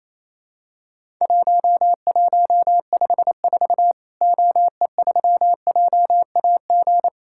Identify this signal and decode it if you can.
Morse code, heard as a keyed tone, about 28 wpm, 700 Hz. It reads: 1154 OE3JAG
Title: Worked All Europe DX Contest - QTC Exchange - 28wpm